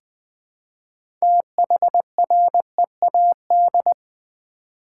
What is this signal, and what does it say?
Morse code, heard as a keyed tone, about 20 wpm, 700 Hz. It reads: THREAD
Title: 6 Letter Words - First 3 Letters Given - 20wpm